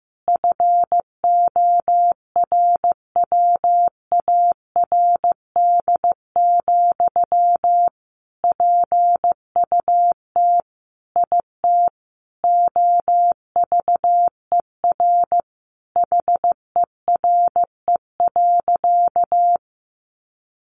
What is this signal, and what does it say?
Morse code, heard as a keyed tone, about 15 wpm, 700 Hz. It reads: FORWARD, PUT IT OVER HERE.
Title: Sentences from Top 1500 Words 15wpm